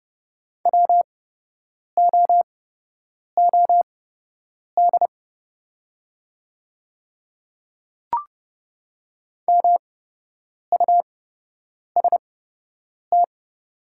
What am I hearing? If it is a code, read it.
Morse code, heard as a keyed tone, about 30 wpm, spaced as for 8 wpm, 700 Hz. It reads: WOOD MUST
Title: 8x Character Spacing - Top 300 Words - 30wpm